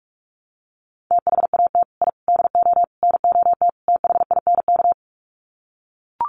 Morse code, heard as a keyed tone, about 44 wpm, 700 Hz. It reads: THAT I DO NOT THINK
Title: Sets of 5 Words Encore 45wpm